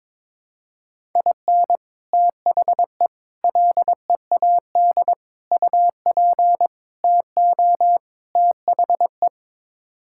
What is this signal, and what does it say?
Morse code, heard as a keyed tone, about 22 wpm, 700 Hz. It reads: IN THE LEAD UP TO THE
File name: practice / 22wpm / Sets of 6 Words - 22wpm